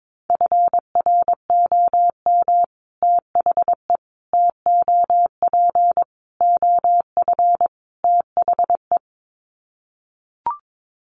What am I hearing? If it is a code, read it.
Morse code, heard as a keyed tone, about 22 wpm, 700 Hz. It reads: FROM THE TOP OF THE
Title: Sets of 5 Words Encore 22wpm